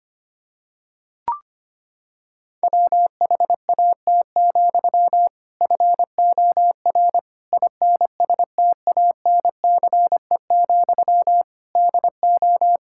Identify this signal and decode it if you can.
Morse code, heard as a keyed tone, about 25 wpm, 700 Hz. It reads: WHAT, FOR INSTANCE, DO
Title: Sentences from Top 1000 Words - Review Part 5 - 25wpm